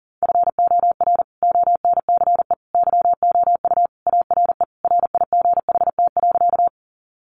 Morse code, heard as a keyed tone, about 40 wpm, 700 Hz. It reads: FOR ONCE YOU ARE RIGHT.
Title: Sentences from Top 300 Words 40wpm